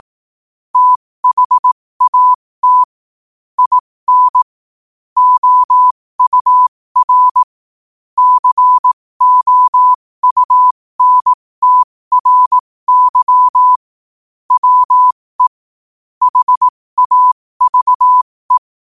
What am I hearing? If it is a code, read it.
Morse code, heard as a keyed tone, about 18 wpm, spaced as for 15 wpm, 1000 Hz. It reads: THAT IN OUR COUNTRY WE HAVE
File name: qotdfiles / 15wpm